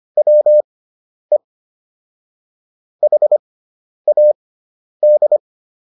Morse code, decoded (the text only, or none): WE HAD